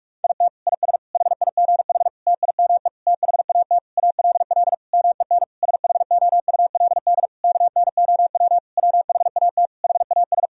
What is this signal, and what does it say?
Morse code, heard as a keyed tone, about 41 wpm, 700 Hz. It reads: IT IS HIGH TIME THAT ALL MEN SHOULD KNOW WHAT HAS
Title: Sentences from Top 1000 Words - Review Part 1 - 45wpm